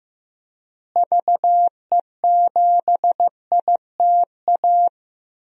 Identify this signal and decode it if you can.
Morse code, heard as a keyed tone, about 15 wpm, 700 Hz. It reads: VE7ITA
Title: DXCC-100 Call Signs - 6 Characters - Repeated 2x - 15wpm